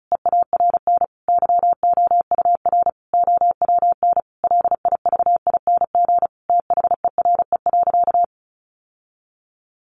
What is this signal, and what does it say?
Morse code, heard as a keyed tone, about 34 wpm, 700 Hz. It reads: EARN YOUR OWN LIVING THERE.
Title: Sentences from Top 1500 Words 35wpm